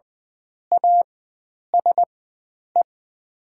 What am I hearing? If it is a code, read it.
Morse code, heard as a keyed tone, about 20 wpm, spaced as for 9 wpm, 700 Hz. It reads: ASE
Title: 4x Character Spacing - Top 500 Words - 20wpm